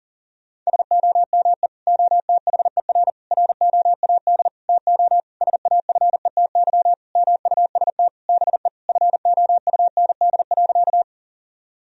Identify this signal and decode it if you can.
Morse code, heard as a keyed tone, about 40 wpm, 700 Hz. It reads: SOME OTHER ROAD TO SAFETY MUST BE FOUND.
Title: Sentences from Top 1000 Words - Review Part 4 - 40wpm